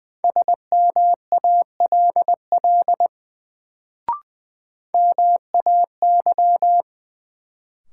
Morse code, heard as a keyed tone, about 20 wpm, 700 Hz. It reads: SMALL MAY
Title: Sets of 2 Words - New Word Drill - 20wpm